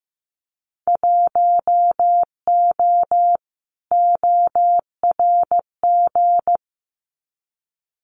Morse code, heard as a keyed tone, about 15 wpm, 700 Hz. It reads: 1O ORG
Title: Winter Field Day Exchanges 15wpm